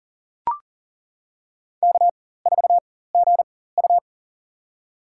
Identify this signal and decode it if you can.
Morse code, heard as a keyed tone, about 40 wpm, spaced as for 19 wpm, 700 Hz. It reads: K4GU
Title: USA Call Signs - 4x Character Spacing - 40wpm